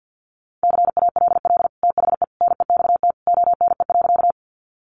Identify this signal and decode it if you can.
Morse code, heard as a keyed tone, about 48 wpm, 700 Hz. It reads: CALL THE NEXT ONE.
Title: Sentences from Top 300 Words 50wpm